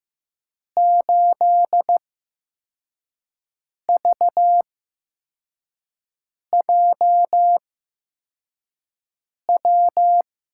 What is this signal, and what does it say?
Morse code, heard as a keyed tone, about 15 wpm, spaced as for 4 wpm, 700 Hz. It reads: 8VJW
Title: USA Call Signs - 6 Characters - 8x Character Spacing - 15wpm